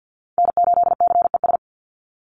Morse code, heard as a keyed tone, about 50 wpm, 700 Hz. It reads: N8CES
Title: USA Call Signs 50wpm